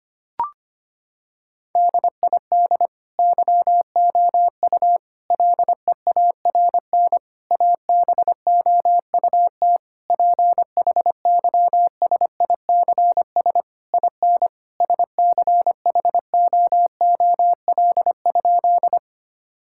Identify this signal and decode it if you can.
Morse code, heard as a keyed tone, about 25 wpm, 700 Hz. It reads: DID YOU LEARN ABOUT PHYSICS IN SCHOOL?
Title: Sentences from Top 1000 Words - Review Part 5 - 25wpm